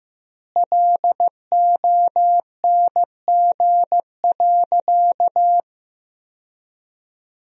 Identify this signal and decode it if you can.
Morse code, heard as a keyed tone, about 15 wpm, 700 Hz. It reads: LONG.
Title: 2x Word-Spacing - Sentences from Top 100 Words - 15wpm